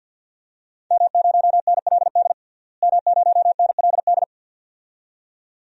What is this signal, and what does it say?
Morse code, heard as a keyed tone, about 52 wpm, 700 Hz. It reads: M0NLD M0NLD
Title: DXCC-100 Call Signs - 5 Characters - Repeated 2x - 50wpm